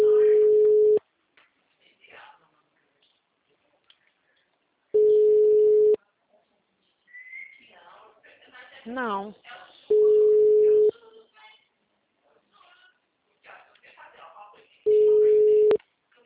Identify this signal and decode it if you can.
Ringback tone (Europe, 425 Hz), with speech.
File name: out- -2014- - - .44